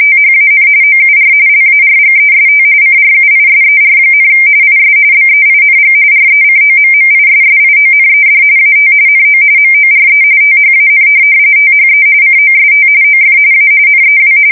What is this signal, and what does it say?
RTTY, about 45.45 baud. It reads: THAN OUR LARGEST CRUISER, IS BEING RECOMMISSIONED AT NO MORE THAN THE COST OF A NEW
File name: fsk170 / USS-New-Jersey